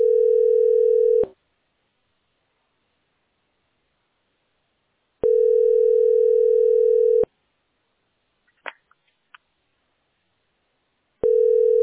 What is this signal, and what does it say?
Ringback tone (North America), 440 + 480 Hz.